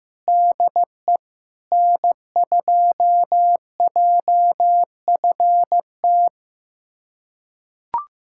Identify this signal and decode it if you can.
Morse code, heard as a keyed tone, about 15 wpm, 700 Hz. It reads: DE N2JFT
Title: USA Callsign DE Callsign 15wpm